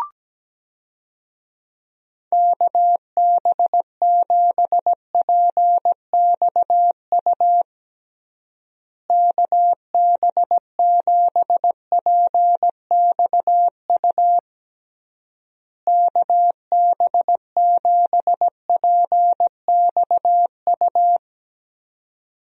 Morse code, heard as a keyed tone, about 17 wpm, 700 Hz. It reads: KB7PXU KB7PXU KB7PXU
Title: USA Call Signs - 6 Characters - Repeated 3x - 17wpm